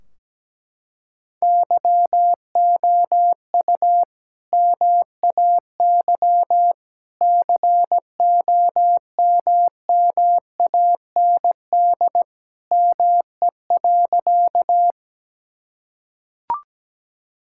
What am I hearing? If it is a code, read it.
Morse code, heard as a keyed tone, about 17 wpm, 700 Hz. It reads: YOU MAY COMMAND ME.
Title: Sentences from Top 1600 Words 17wpm